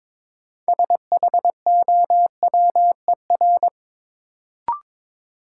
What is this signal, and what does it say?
Morse code, heard as a keyed tone, about 22 wpm, 700 Hz. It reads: SHOWER
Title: Sentences from Top 1600 Words - New Word Drill Alt - 22wpm